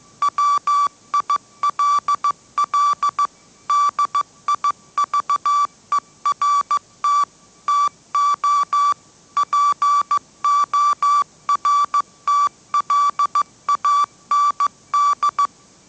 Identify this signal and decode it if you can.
Morse code, heard as a keyed tone, about 15 wpm, 1160 Hz. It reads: WILL DIVERT TO PORTLAND
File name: mrx / mrx066 16wpm